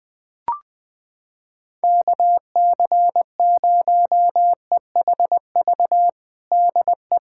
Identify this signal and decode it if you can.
Morse code, heard as a keyed tone, about 20 wpm, 700 Hz. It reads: KC0EHV DE
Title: USA Callsign DE Callsign - 6 Characters - 20wpm